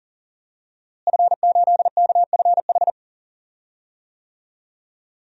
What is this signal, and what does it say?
Morse code, heard as a keyed tone, about 40 wpm, 700 Hz. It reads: F8KFH
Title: DXCC-100 Call Signs - 5 Characters - Repeated 2x - 40wpm